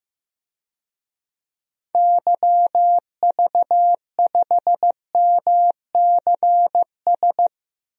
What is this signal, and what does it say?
Morse code, heard as a keyed tone, about 15 wpm, 700 Hz. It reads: YV5MCS
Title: DXCC-100 Call Signs - 6 Characters - Repeated 2x - 15wpm